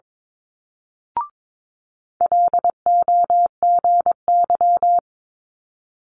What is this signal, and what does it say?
Morse code, heard as a keyed tone, about 22 wpm, 700 Hz. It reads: LOGY
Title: Multi-letter Phonemes Quick Fire 22wpm